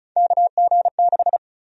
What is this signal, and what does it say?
Morse code, heard as a keyed tone, about 36 wpm, 700 Hz. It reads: KG6
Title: USA Call Sign Prefixes - Repeated 2x - 35wpm